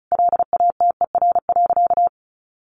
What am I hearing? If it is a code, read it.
Morse code, heard as a keyed tone, about 34 wpm, 700 Hz. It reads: LATER.
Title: Sentences from Top 1200 Words 35wpm